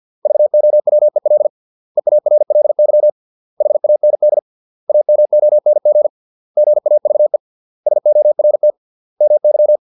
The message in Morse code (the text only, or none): VOWEL EARLY HAND AMONG GAVE SORT MY